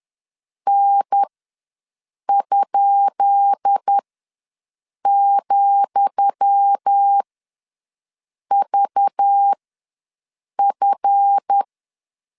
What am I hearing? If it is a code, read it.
Morse code, heard as a keyed tone, about 10 wpm, spaced as for 6 wpm, 800 Hz. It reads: N?,VF
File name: LEZIONI / 10 SIGNS / 10D COMA ws 10wpm 4314